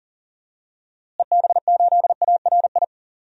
Morse code, heard as a keyed tone, about 40 wpm, 700 Hz. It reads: EB8ARI
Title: DXCC-100 Call Signs - 6 Characters - Repeated 3x - 40wpm